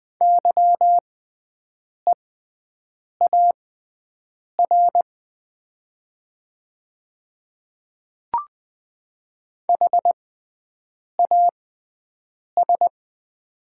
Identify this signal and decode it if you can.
Morse code, heard as a keyed tone, about 20 wpm, spaced as for 7 wpm, 700 Hz. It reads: YEAR HAS
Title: 6x Character Spacing - Top 200 Words - 20wpm